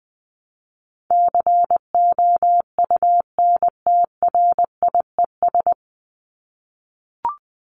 Morse code, heard as a keyed tone, about 20 wpm, 700 Hz. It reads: COUNTRIES
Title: ES:S Suffix - Top 100 Words with Base Words 20wpm